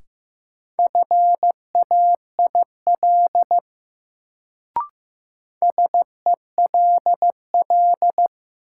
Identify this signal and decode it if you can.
Morse code, heard as a keyed tone, about 15 wpm, 700 Hz. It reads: FAIL SELL
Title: Sentences from Top 1100 Words - New Word Drill Alt - 15wpm